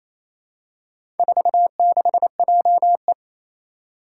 Morse code, heard as a keyed tone, about 27 wpm, 700 Hz. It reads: 46JE
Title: Four Challenging Letter-Number Combinations 28wpm